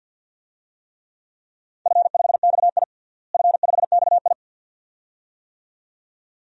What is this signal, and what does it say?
Morse code, heard as a keyed tone, about 52 wpm, 700 Hz. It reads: F5XI F5XI
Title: DXCC-100 Call Signs - 4 Characters - Repeated 2x - 50wpm